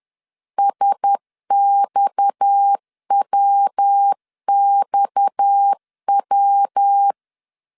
Morse code, heard as a keyed tone, about 10 wpm, 800 Hz. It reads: SXWXW